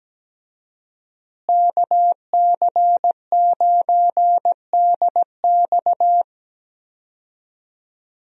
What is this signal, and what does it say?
Morse code, heard as a keyed tone, about 17 wpm, 700 Hz. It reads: KC9DX
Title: USA Call Signs - 5 Characters - Repeated 2x - 17wpm